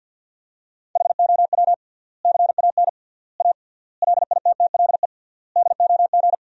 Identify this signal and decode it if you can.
Morse code, heard as a keyed tone, about 52 wpm, 700 Hz. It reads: HOW CAN A LITTLE DOG
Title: 3x Word-Spacing - Sentences from Top 500 Words - 50wpm